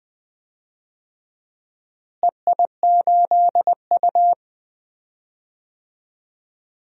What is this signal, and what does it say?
Morse code, heard as a keyed tone, about 20 wpm, 700 Hz. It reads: EI8U
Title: DXCC-100 Call Signs - 4 Characters - Repeated 2x - 20wpm